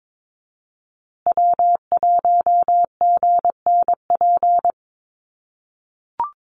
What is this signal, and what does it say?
Morse code, heard as a keyed tone, about 22 wpm, 700 Hz. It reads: W1GNP
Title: USA Call Signs Repeated 3x 22wpm